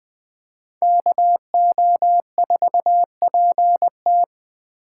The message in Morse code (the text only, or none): KO4PT